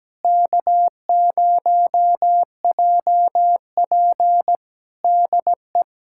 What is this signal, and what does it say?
Morse code, heard as a keyed tone, about 17 wpm, 700 Hz. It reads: K0JP DE
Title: Practice QSO - Exchange 4 - Ending - 17wpm